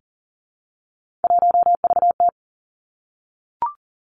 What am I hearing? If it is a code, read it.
Morse code, heard as a keyed tone, about 39 wpm, 700 Hz. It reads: 1VT
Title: Three Challenging Letter-Number Combinations 40wpm